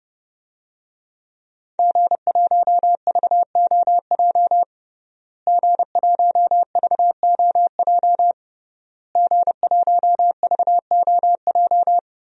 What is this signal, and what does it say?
Morse code, heard as a keyed tone, about 30 wpm, 700 Hz. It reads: G1VOJ G1VOJ G1VOJ
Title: DXCC-100 Call Signs - 5 Characters - Repeated 3x - 30wpm